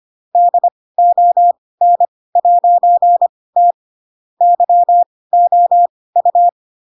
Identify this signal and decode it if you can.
Morse code, heard as a keyed tone, about 25 wpm, spaced as for 18 wpm, 700 Hz. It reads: DON'T YOU
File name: Pinocchio / Pinocchio ch4 12 25 ik1hge